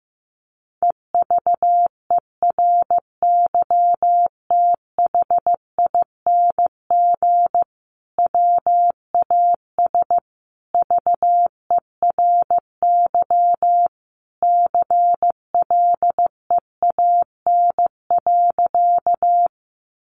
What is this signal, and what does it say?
Morse code, heard as a keyed tone, about 15 wpm, 700 Hz. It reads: EVERYTHING WAS VERY CLEAN.
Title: Sentences from Top 1100 Words 15wpm